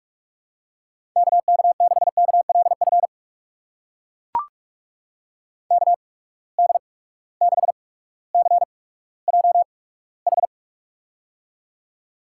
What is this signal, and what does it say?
Morse code, heard as a keyed tone, about 44 wpm, spaced as for 19 wpm, 700 Hz. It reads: KK6KLF KD6CJH
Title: USA Call Signs - 8x Character Spacing - 45wpm